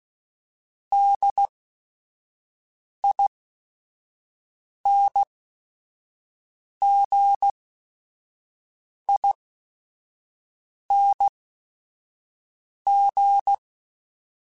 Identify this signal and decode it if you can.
Morse code, heard as a keyed tone, about 16 wpm, spaced as for 5 wpm, 800 Hz. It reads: DINGING